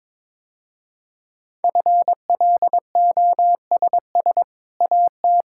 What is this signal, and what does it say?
Morse code, heard as a keyed tone, about 22 wpm, 700 Hz. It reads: FLOSS AT
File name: practice / 22wpm / Easy Sentences with Hints - 22wpm